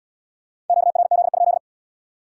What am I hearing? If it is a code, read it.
Morse code, heard as a keyed tone, about 70 wpm, 700 Hz. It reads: QRZ?